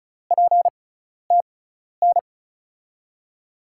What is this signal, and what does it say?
Morse code, heard as a keyed tone, about 36 wpm, spaced as for 12 wpm, 700 Hz. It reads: PTN
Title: USA Call Signs - 5 Characters - 6x Character Spacing - 35wpm